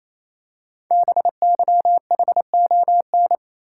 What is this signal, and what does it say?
Morse code, heard as a keyed tone, about 28 wpm, 700 Hz. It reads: BYHON
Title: Code Groups - 5 Letters - 28wpm